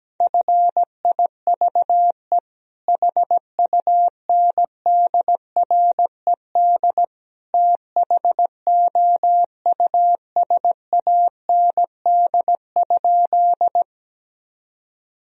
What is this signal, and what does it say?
Morse code, heard as a keyed tone, about 17 wpm, 700 Hz. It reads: FIVE HUNDRED THOUSAND?
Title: Sentences from Top 1000 Words - Review Part 3 - 17wpm